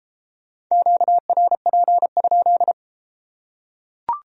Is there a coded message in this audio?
Yes: Morse code from 0.7 to 2.7 s, heard as a keyed tone, about 29 wpm, 700 Hz. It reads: QRP?